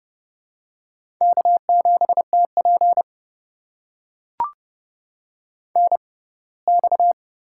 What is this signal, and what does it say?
Morse code, heard as a keyed tone, about 30 wpm, spaced as for 20 wpm, 700 Hz. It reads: K7TP NX
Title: USA Call Signs - 4 Characters - 6x Character Spacing - 30wpm